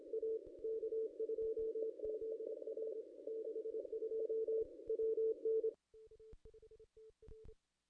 Morse code, heard as a keyed tone, about 25 wpm, 440 Hz. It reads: A K2ZR 62 WN
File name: contests / ss-cw-k5tr-op-n5rz -1500